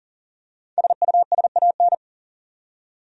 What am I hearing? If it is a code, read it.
Morse code, heard as a keyed tone, about 40 wpm, 700 Hz. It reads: SUSAN
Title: USA Common Names - 40wpm